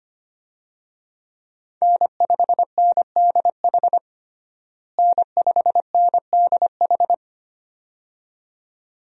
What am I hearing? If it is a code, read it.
Morse code, heard as a keyed tone, about 25 wpm, 700 Hz. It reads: N5NDH N5NDH
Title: USA Call Signs - 5 Characters - Repeated 2x - 25wpm